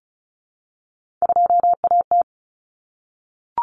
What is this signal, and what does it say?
Morse code, heard as a keyed tone, about 34 wpm, 700 Hz. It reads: 2AT
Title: Common Cut Numbers - 3-Digit 35wpm